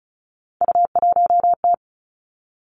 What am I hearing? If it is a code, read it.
Morse code, heard as a keyed tone, about 36 wpm, 700 Hz. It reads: U1T
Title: Three Challenging Letter-Number Combinations 35wpm